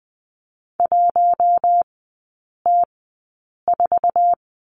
Morse code, heard as a keyed tone, about 20 wpm, spaced as for 8 wpm, 700 Hz. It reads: 1T4
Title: 047 Lesson 9e - Number 4 - Three Characters 20wpm